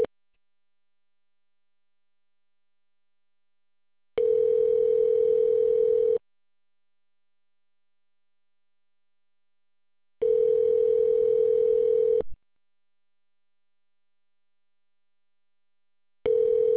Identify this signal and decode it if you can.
Ringback tone (North America), 440 + 480 Hz.